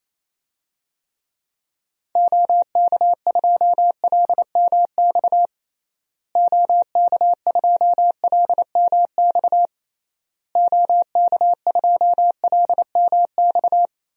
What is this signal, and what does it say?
Morse code, heard as a keyed tone, about 28 wpm, 700 Hz. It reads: OK2LMX OK2LMX OK2LMX
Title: DXCC-100 Call Signs - 6 Characters - Repeated 3x - 28wpm